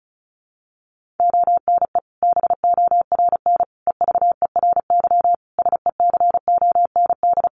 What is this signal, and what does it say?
Morse code, heard as a keyed tone, about 34 wpm, 700 Hz. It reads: ONE BORN EVERY SECOND
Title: Sentences from Top 1400 Words 35wpm